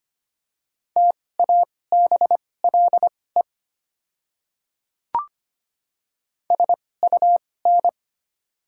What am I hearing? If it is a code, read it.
Morse code, heard as a keyed tone, about 25 wpm, spaced as for 18 wpm, 700 Hz. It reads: TABLE SUN
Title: 2x Character Spacing - Top 500 Words - 25wpm